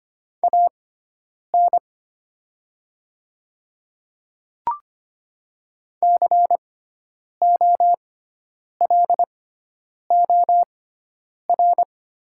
Morse code, heard as a keyed tone, about 25 wpm, spaced as for 9 wpm, 700 Hz. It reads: AN COLOR
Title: 6x Character Spacing - Top 300 Words - 25wpm